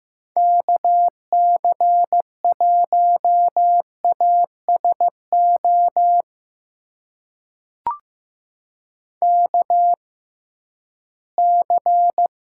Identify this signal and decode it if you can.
Morse code, heard as a keyed tone, about 15 wpm, spaced as for 11 wpm, 700 Hz. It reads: KC1ASO KC
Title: USA Call Signs - 6 Characters - 6x Character Spacing - 15wpm